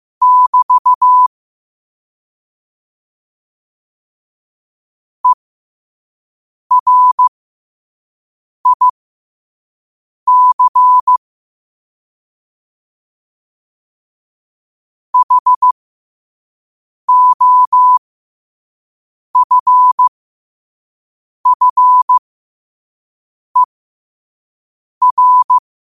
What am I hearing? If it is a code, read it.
Morse code, heard as a keyed tone, about 15 wpm, spaced as for 5 wpm, 1000 Hz. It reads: <BT> ERIC HOFFER
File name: qotdfiles / 5wpm